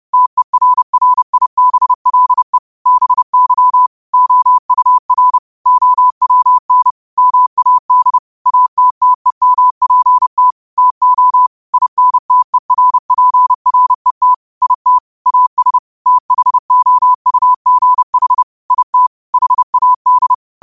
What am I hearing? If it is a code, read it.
Morse code, heard as a keyed tone, about 30 wpm, 1000 Hz. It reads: TERRIBLE BY OUR OWN MAD ATTEMPT TO INTERPRET IT AS THOUGH IT HAD
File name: qotdfiles / 30wpm